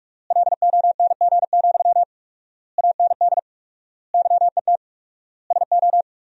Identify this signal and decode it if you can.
Morse code, heard as a keyed tone, about 42 wpm, 700 Hz. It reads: LONG, AND YET SO
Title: 4x Word-Spacing - Sentences from Top 500 Words - 45wpm